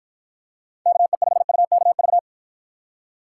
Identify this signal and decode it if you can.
Morse code, heard as a keyed tone, about 54 wpm, 700 Hz. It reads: KE5UKV